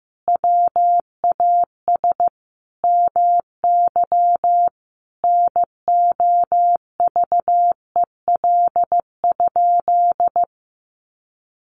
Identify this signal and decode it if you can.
Morse code, heard as a keyed tone, about 15 wpm, 700 Hz. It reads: WAS MY NOVEL?
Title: Sentences from Top 1300 Words 15wpm